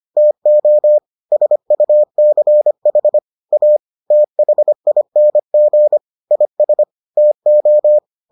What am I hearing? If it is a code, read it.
Morse code, heard as a keyed tone, about 25 wpm, 600 Hz. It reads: TO SUCH A THING IS TO